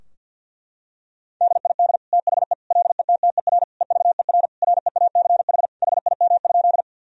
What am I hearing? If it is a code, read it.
Morse code, heard as a keyed tone, about 50 wpm, 700 Hz. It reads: DID THE LETTER EVER REACH HIM?
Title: Sentences from Top 1000 Words - Review Part 2 - 50wpm